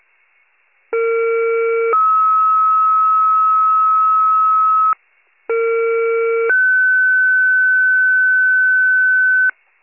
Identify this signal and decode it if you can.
Two-tone page, heard as a one-second tone, then a three-second tone, about 460 then 1290 Hz, about 460 then 1500 Hz, with noise. It is heard from 0.9 to 9.5 s.